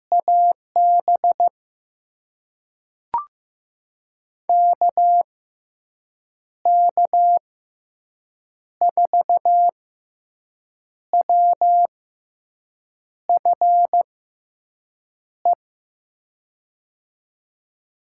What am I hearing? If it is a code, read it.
Morse code, heard as a keyed tone, about 15 wpm, spaced as for 6 wpm, 700 Hz. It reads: AB KK4WFE
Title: USA Call Signs - 6 Characters - 6x Character Spacing - 15wpm